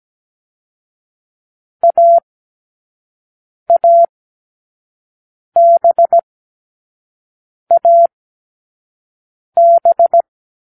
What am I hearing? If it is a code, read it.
Morse code, heard as a keyed tone, about 17 wpm, 680 Hz. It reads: A A B A B